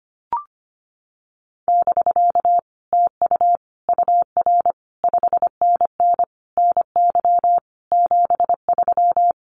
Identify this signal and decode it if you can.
Morse code, heard as a keyed tone, about 25 wpm, 700 Hz. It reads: <BK> TU UR 5NN NY 73
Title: Common QSO Phrases Encore 25wpm